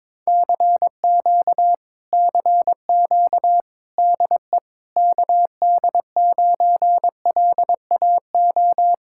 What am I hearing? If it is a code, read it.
Morse code, heard as a keyed tone, about 22 wpm, 700 Hz. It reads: CQ CQ DE KD9LAO
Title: Speed-Racing Common QSO Phrases Encore - 22wpm